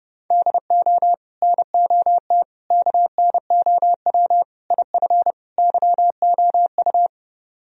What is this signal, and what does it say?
Morse code, heard as a keyed tone, about 32 wpm, 700 Hz. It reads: DO NOT KNOW IF YOU